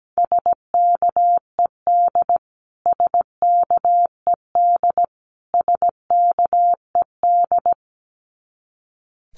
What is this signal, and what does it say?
Morse code, heard as a keyed tone, about 17 wpm, 700 Hz. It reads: SKED SKED SKED
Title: Abbreviations Repeated 3x - 17wpm